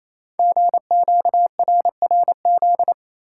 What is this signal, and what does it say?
Morse code, heard as a keyed tone, about 29 wpm, 700 Hz. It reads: GQRRZ
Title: Code Groups - 5 Letters - 28wpm